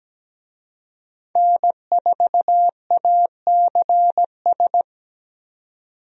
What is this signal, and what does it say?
Morse code, heard as a keyed tone, about 17 wpm, 700 Hz. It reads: N4ACS
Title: USA Call Signs - 5 Characters - 17wpm